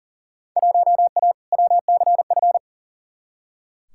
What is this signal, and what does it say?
Morse code, heard as a keyed tone, about 40 wpm, 700 Hz. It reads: 1A WCF
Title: Field Day Exchanges - 40wpm